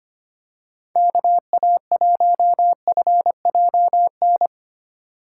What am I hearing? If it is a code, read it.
Morse code, heard as a keyed tone, about 25 wpm, 700 Hz. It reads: KA1FJN
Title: USA Call Signs - 6 Characters - 3x Character Spacing - 25wpm